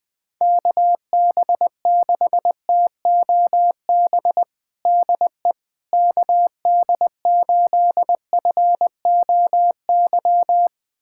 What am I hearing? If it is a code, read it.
Morse code, heard as a keyed tone, about 20 wpm, 700 Hz. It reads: KB6TOB DE KD8FOY
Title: USA Callsign DE Callsign 20wpm